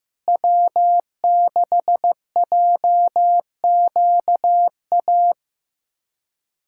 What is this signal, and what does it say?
Morse code, heard as a keyed tone, about 15 wpm, 700 Hz. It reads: W6JQA
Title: SOTA QSO Practice 15wpm